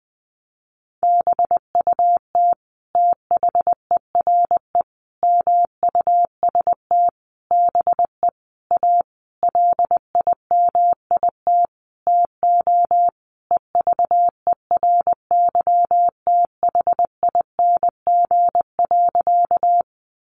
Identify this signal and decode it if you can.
Morse code, heard as a keyed tone, about 20 wpm, 700 Hz. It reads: BUT THERE MUST BE A LIMIT TO EVERYTHING.
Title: Sentences from Top 1500 Words 20wpm